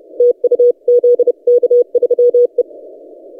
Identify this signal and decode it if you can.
Morse code, heard as a keyed tone, about 29 wpm, 490 Hz. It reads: TU ZK3E
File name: posts / zk3e-30cw